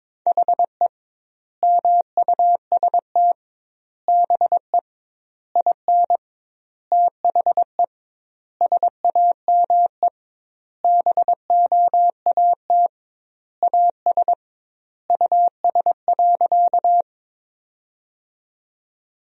Morse code, heard as a keyed tone, about 22 wpm, 700 Hz. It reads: HE MUST BE IN THE SAME BOAT AS US.
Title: 2x Word-Spacing - Sentences from Top 500 Words - 22wpm